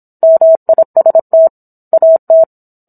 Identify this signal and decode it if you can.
Morse code, heard as a keyed tone, about 27 wpm, 650 Hz. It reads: MIST AT